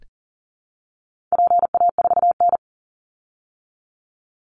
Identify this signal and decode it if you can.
Morse code, heard as a keyed tone, about 40 wpm, 700 Hz. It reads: PA4N
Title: Call Signs Repeated 2x - 40wpm